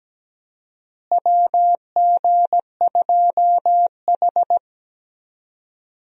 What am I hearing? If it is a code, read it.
Morse code, heard as a keyed tone, about 17 wpm, 700 Hz. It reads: WG2H
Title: USA Call Signs - 4 Characters - Repeated 2x - 17wpm